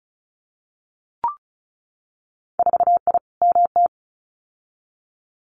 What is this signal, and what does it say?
Morse code, heard as a keyed tone, about 36 wpm, 700 Hz. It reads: 4I MT
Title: Winter Field Day Exchanges Encore 35wpm